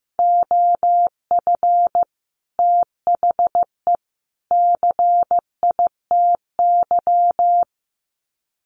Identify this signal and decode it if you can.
Morse code, heard as a keyed tone, about 15 wpm, 700 Hz. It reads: OF THE CITY
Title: Sets of 3 Words 15wpm